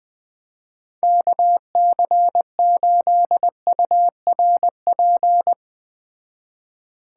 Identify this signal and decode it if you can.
Morse code, heard as a keyed tone, about 20 wpm, 700 Hz. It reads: KC8URP
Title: Speed Racing - USA 2x3 Call Signs 17wpm